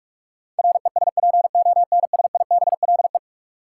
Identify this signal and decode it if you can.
Morse code, heard as a keyed tone, about 46 wpm, 700 Hz. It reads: RESPONSIBLE
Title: Sentences from Top 1000 Words - Review Part 7 - New Word Drill - 45wpm